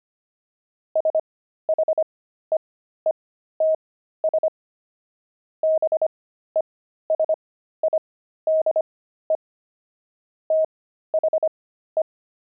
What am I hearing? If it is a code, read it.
Morse code, heard as a keyed tone, about 25 wpm, spaced as for 13 wpm, 630 Hz. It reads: SHEETS BESIDE THE
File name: pangram-fs-13-cs-25-beehive